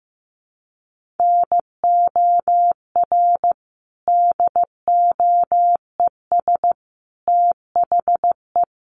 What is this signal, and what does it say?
Morse code, heard as a keyed tone, about 15 wpm, 700 Hz. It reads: NOR DOES THE
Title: Sentences from Top 1500 Words 15wpm